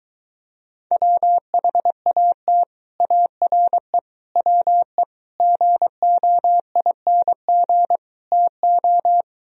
Morse code, heard as a keyed tone, about 23 wpm, 700 Hz. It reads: WHAT ARE WE GOING TO